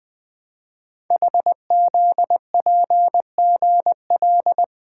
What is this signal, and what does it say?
Morse code, heard as a keyed tone, about 20 wpm, 700 Hz. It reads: HZPGL
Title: Code Groups - 5 Letters - 20wpm